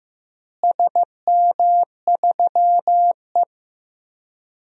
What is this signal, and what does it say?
Morse code, heard as a keyed tone, about 16 wpm, 700 Hz. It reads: SM3E
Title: DXCC-100 Call Signs - 4 Characters - Repeated 3x - 15wpm